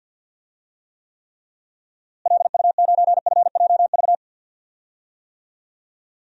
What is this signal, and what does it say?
Morse code, heard as a keyed tone, about 52 wpm, 700 Hz. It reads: LU9FJV
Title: DXCC-100 Call Signs - 6 Characters - Repeated 3x - 50wpm